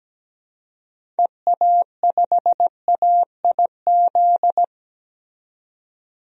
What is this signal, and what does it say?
Morse code, heard as a keyed tone, about 17 wpm, 700 Hz. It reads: EA5AIZ
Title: DXCC-100 Call Signs - 6 Characters - 17wpm